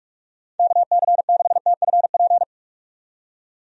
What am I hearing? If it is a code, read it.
Morse code, heard as a keyed tone, about 46 wpm, 700 Hz. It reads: KC6TFP
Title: USA Call Signs - 6 Characters - 45wpm